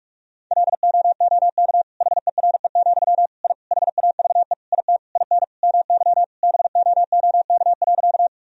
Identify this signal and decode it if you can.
Morse code, heard as a keyed tone, about 46 wpm, 700 Hz. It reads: LOOK HERE, I HAVE IT IN MY BOOK.
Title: Sentences from Top 1000 Words - Review Part 1 - 45wpm